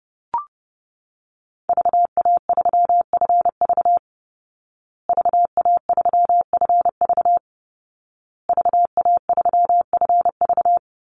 Morse code, heard as a keyed tone, about 28 wpm, 700 Hz. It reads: VA3FV VA3FV VA3FV
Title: Call Signs Repeated 2x - 30wpm